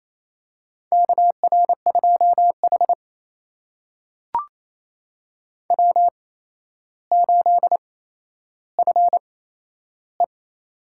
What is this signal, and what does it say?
Morse code, heard as a keyed tone, about 28 wpm, spaced as for 12 wpm, 700 Hz. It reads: KR2H W8FE
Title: USA Call Signs - 4 Characters - 8x Character Spacing - 28wpm